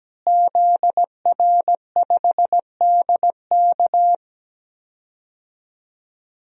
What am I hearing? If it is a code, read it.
Morse code, heard as a keyed tone, about 17 wpm, 700 Hz. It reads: ZR5DK
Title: DXCC-100 Call Signs - 5 Characters - Repeated 2x - 17wpm